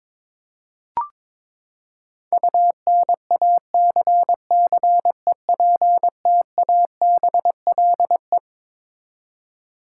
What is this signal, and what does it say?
Morse code, heard as a keyed tone, about 22 wpm, 700 Hz. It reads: UNACCEPTABLE
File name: Twelve Letter Words - 22wpm